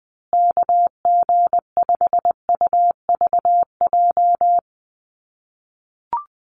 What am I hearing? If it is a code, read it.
Morse code, heard as a keyed tone, about 20 wpm, 700 Hz. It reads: KG5UVJ
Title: USA Call Signs Repeated 3x 20wpm